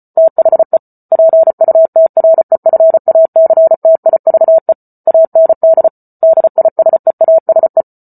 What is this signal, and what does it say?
Morse code, heard as a keyed tone, about 34 wpm, 650 Hz. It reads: THE PUTREFACTIVE AND DISEASE